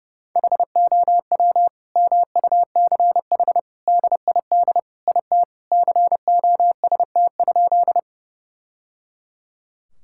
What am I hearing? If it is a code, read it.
Morse code, heard as a keyed tone, about 30 wpm, 700 Hz. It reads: HOW MUCH DID IT COST?
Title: Sentences from Top 1000 Words - Review Part 5 - 30wpm